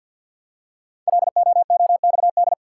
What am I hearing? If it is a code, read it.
Morse code, heard as a keyed tone, about 52 wpm, 700 Hz. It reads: LOOXD
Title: Code Groups - 5 Letter:Numbers - 50wpm